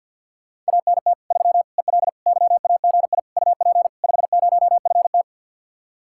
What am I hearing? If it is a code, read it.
Morse code, heard as a keyed tone, about 67 wpm, spaced as for 58 wpm, 700 Hz. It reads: ANT 3 EL YAGI UP 50FT
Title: Speed-Racing Common QSO Phrases Encore - 50wpm